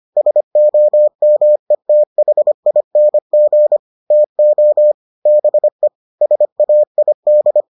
Morse code, heard as a keyed tone, about 25 wpm, 600 Hz. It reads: SOMETHING TO BE SAID